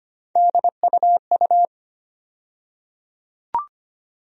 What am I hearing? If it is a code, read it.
Morse code, heard as a keyed tone, about 25 wpm, 700 Hz. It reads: DUU
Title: USA Call Sign Postfixes - 3 Letters 25wpm